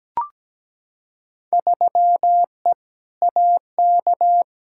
Morse code, heard as a keyed tone, about 17 wpm, 700 Hz. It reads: 3E AK
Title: Field Day Exchanges Encore - 17wpm